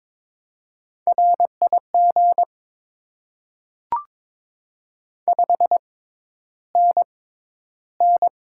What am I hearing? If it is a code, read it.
Morse code, heard as a keyed tone, about 22 wpm, spaced as for 11 wpm, 700 Hz. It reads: RIG 5NN
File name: Top 20 QSO Elements - 6x Character Spacing - 22wpm